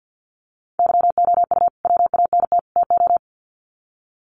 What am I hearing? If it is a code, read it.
Morse code, heard as a keyed tone, about 50 wpm, 700 Hz. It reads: YOU WANT TO
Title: Sets of 3 Words 50wpm